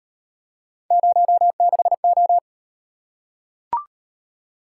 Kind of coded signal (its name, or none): Morse code